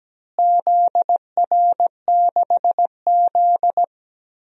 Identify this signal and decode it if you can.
Morse code, heard as a keyed tone, about 17 wpm, 700 Hz. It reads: ZR6Z
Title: DXCC-100 Call Signs - 4 Characters - Repeated 3x - 17wpm